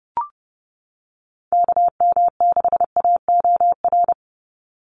Morse code, heard as a keyed tone, about 30 wpm, 700 Hz. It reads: KM6AOR
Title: USA Call Signs 30wpm